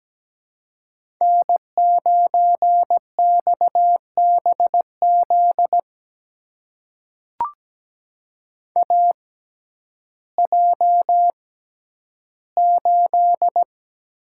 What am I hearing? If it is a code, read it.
Morse code, heard as a keyed tone, about 17 wpm, spaced as for 10 wpm, 700 Hz. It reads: N9XBZ AJ8
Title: USA Call Signs - 6x Character Spacing - 17wpm